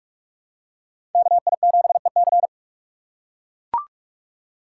Morse code, heard as a keyed tone, about 42 wpm, 700 Hz. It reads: KI7EC
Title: USA Call Signs - 5 Characters - 4x Character Spacing - 45wpm